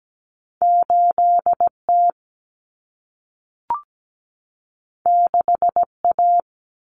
Morse code, heard as a keyed tone, about 17 wpm, 700 Hz. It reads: 8T 6A
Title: Common Cut Numbers - 2-Digit 17wpm